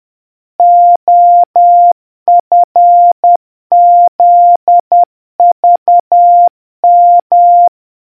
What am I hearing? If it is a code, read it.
Morse code, heard as a keyed tone, about 10 wpm, 700 Hz. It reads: OFZVM